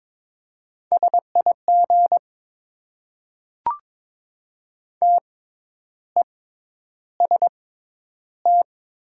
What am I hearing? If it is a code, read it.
Morse code, heard as a keyed tone, about 22 wpm, spaced as for 10 wpm, 700 Hz. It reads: SIG TEST
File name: Top 90 QSO Elements - 6x Character Spacing - 22wpm